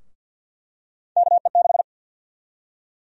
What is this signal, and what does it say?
Morse code, heard as a keyed tone, about 48 wpm, 700 Hz. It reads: KE6
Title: USA Call Sign Prefixes - Repeated 2x - 50wpm